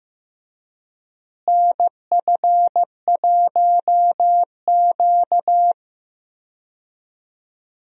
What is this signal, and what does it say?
Morse code, heard as a keyed tone, about 15 wpm, 700 Hz. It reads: NF1Q
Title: USA Call Signs - 4 Characters - Repeated 3x - 15wpm